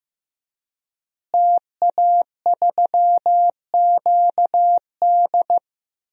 Morse code, heard as a keyed tone, about 15 wpm, 700 Hz. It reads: TA3QD
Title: DXCC-100 Call Signs - 5 Characters - Repeated 3x - 15wpm